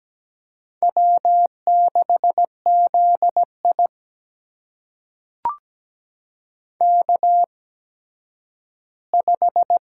Morse code, heard as a keyed tone, about 17 wpm, spaced as for 10 wpm, 700 Hz. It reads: W6ZI K5
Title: USA Call Signs - 4 Characters - 8x Character Spacing - 17wpm